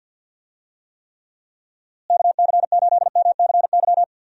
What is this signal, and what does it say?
Morse code, heard as a keyed tone, about 52 wpm, 700 Hz. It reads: KC8MCY
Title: USA Call Signs - 6 Characters - Repeated 3x - 50wpm